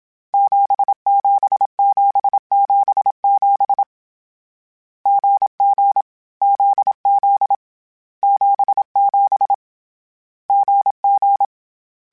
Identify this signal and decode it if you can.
Morse code, heard as a keyed tone, about 27 wpm, 800 Hz. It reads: 77777 GG ZZ 77 GG
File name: CW Code Groups-mp3s / Sets mp3s / GZ GZ7 sp @25